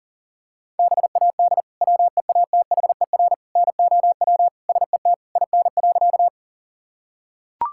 Morse code, heard as a keyed tone, about 37 wpm, 690 Hz. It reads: BAD WEATHER NOW SET IN.
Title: Sentences from Top 1000 Words - Review Part 5 - 40wpm